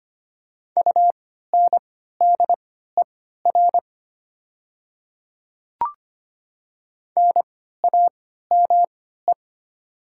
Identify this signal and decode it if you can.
Morse code, heard as a keyed tone, about 25 wpm, spaced as for 14 wpm, 700 Hz. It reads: UNDER NAME
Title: 3x Character Spacing - Top 500 Words - 25wpm